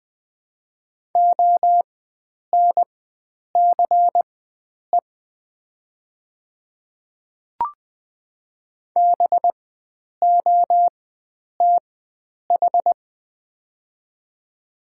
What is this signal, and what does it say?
Morse code, heard as a keyed tone, about 21 wpm, spaced as for 9 wpm, 700 Hz. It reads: ONCE BOTH
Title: 4x Character Spacing - Top 300 Words - 20wpm